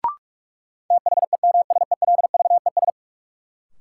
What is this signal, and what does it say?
Morse code, heard as a keyed tone, about 46 wpm, 700 Hz. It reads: THEMSELVES
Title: Sentences from Top 800 Words - New Word Drill - 45wpm